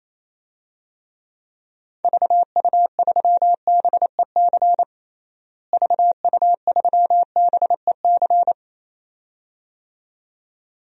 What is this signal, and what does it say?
Morse code, heard as a keyed tone, about 27 wpm, 700 Hz. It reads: VU3BEC VU3BEC
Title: DXCC-100 Call Signs - 6 Characters - Repeated 2x - 28wpm